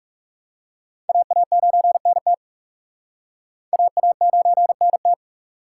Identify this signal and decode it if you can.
Morse code, heard as a keyed tone, about 37 wpm, 700 Hz. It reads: AA9NT AA9NT
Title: Speed Racing - USA 2x2 Call Signs 30wpm